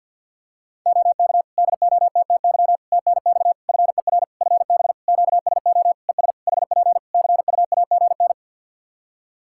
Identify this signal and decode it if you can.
Morse code, heard as a keyed tone, about 51 wpm, 700 Hz. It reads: OK DOTTY TNX FER FB QSO ES HP CUAGN
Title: Practice QSO - Exchange 4 - Ending - 50wpm